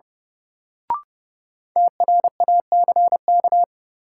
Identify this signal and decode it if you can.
Morse code, heard as a keyed tone, about 32 wpm, 700 Hz. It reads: TRACK
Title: Sentences from Top 1100 Words - New Word Drill Alt - 30wpm